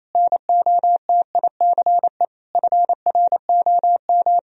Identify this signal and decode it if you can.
Morse code, heard as a keyed tone, about 28 wpm, 700 Hz. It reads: NOTICE FROM
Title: Sentences from Top 1000 Words - Review Part 2 - 28wpm